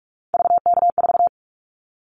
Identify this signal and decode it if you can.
Morse code, heard as a keyed tone, about 46 wpm, 700 Hz. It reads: VK4
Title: Callsign Prefixes for Top 50 DXCC - 45wpm